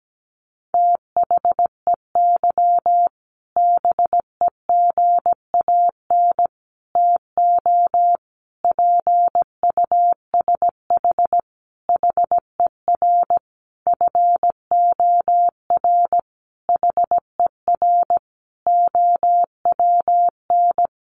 Morse code, heard as a keyed tone, about 17 wpm, 700 Hz. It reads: THEY BEGAN TO PUSH HER FOR HER OWN
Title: Sentences from Top 1300 Words 17wpm